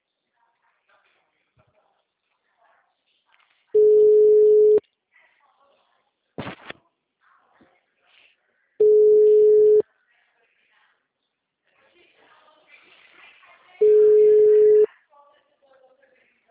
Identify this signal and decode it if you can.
Ringback tone (Europe, 425 Hz), with speech.